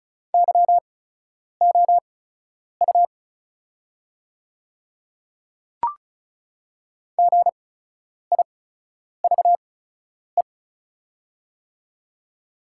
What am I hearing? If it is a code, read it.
Morse code, heard as a keyed tone, about 34 wpm, spaced as for 10 wpm, 700 Hz. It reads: YOU GIVE
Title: 8x Character Spacing - Top 300 Words - 35wpm